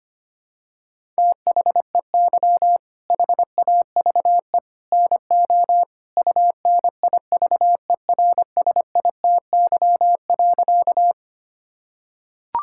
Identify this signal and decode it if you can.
Morse code, heard as a keyed tone, about 25 wpm, 700 Hz. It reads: THEY HAVE NO UNIVERSITY.
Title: Sentences from Top 1000 Words - Review Part 5 - 25wpm